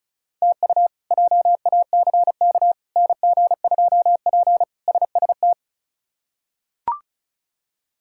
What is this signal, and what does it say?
Morse code, heard as a keyed tone, about 34 wpm, 700 Hz. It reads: TU JACK NG2P SST
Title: K1USN SST QSO Practice - 35wpm